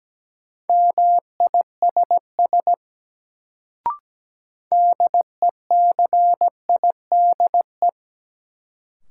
Morse code, heard as a keyed tone, about 17 wpm, 700 Hz. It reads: MISS DECIDE
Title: Sentences from Top 500 Words - New Word Drill - 17wpm